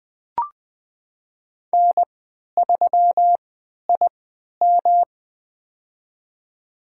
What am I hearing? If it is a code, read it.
Morse code, heard as a keyed tone, about 20 wpm, spaced as for 11 wpm, 700 Hz. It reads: N3IM
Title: USA Call Signs - 4 Characters - 3x Character Spacing - 20wpm